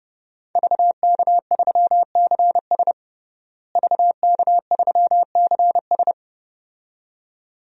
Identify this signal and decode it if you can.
Morse code, heard as a keyed tone, about 30 wpm, 700 Hz. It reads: VK3CS VK3CS
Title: DXCC-100 Call Signs - 5 Characters - Repeated 2x - 30wpm